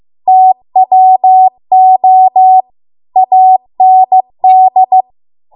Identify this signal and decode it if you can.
Morse code, heard as a keyed tone, about 15 wpm, 750 Hz. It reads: TWO AND